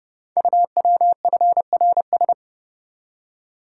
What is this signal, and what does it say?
Morse code, heard as a keyed tone, about 30 wpm, 690 Hz. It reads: UWFRS
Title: Code Groups - 5 Letters - 30wpm